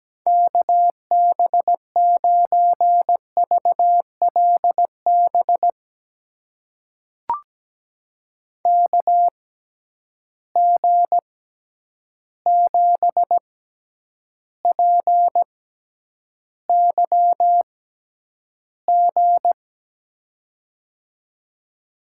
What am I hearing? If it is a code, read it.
Morse code, heard as a keyed tone, about 17 wpm, spaced as for 9 wpm, 700 Hz. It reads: KB9VLB KG7PYG
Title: USA Call Signs - 6 Characters - 6x Character Spacing - 17wpm